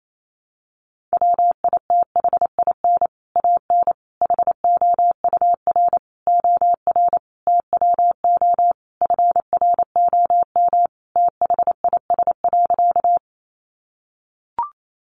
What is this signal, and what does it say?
Morse code, heard as a keyed tone, about 28 wpm, 700 Hz. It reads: WITHIN AN HOUR OR TWO FROM THIS.
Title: Sentences from Top 700 Words 28wpm